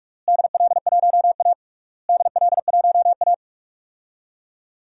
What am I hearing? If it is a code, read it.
Morse code, heard as a keyed tone, about 41 wpm, 700 Hz. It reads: DL1A DL1A
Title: DXCC-100 Call Signs - 4 Characters - Repeated 3x - 45wpm